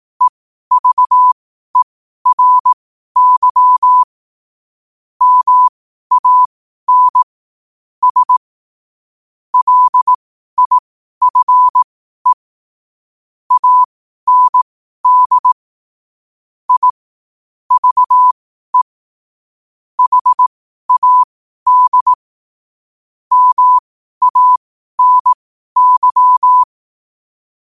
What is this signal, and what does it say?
Morse code, heard as a keyed tone, about 18 wpm, spaced as for 12 wpm, 1000 Hz. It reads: EVERY MAN S LIFE AND I VE HAD MANY
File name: qotdfiles / 12wpm